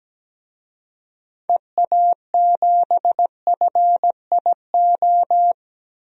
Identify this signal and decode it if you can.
Morse code, heard as a keyed tone, about 17 wpm, 700 Hz. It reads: EA7FIO
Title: DXCC-100 Call Signs - 6 Characters - Repeated 2x - 17wpm